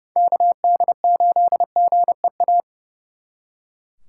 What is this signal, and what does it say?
Morse code, heard as a keyed tone, about 34 wpm, 700 Hz. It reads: KD8GEA
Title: Speed Racing - USA 2x3 Call Signs 30wpm